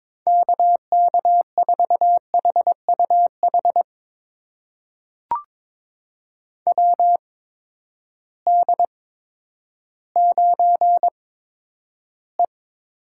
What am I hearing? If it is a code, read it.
Morse code, heard as a keyed tone, about 22 wpm, spaced as for 11 wpm, 700 Hz. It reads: KK4HUH WD9E
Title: USA Call Signs - 6 Characters - 8x Character Spacing - 22wpm